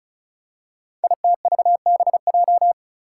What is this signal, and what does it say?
Morse code, heard as a keyed tone, about 36 wpm, 700 Hz. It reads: ITVBJ
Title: Code Groups - 5 Letters - 35wpm